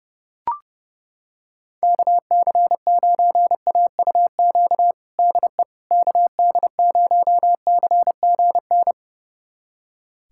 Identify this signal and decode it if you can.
Morse code, heard as a keyed tone, about 30 wpm, 700 Hz. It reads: KC9AUQ DE KD0CGN
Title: USA Callsign DE Callsign - 6 Characters - 30wpm